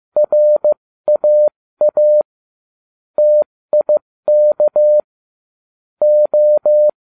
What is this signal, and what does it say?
Morse code, heard as a keyed tone, about 15 wpm, 600 Hz. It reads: RAA TIK O